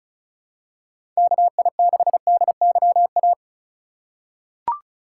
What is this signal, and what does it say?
Morse code, heard as a keyed tone, about 36 wpm, 700 Hz. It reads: KI6DYA
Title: USA Call Signs - 8x Character Spacing - 35wpm